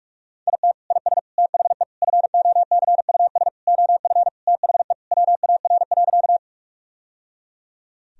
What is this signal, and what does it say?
Morse code, heard as a keyed tone, about 42 wpm, 700 Hz. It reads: IT IS THE FOCUS OF THE WAR.
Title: Sentences from Top 1000 Words - Review Part 4 - 45wpm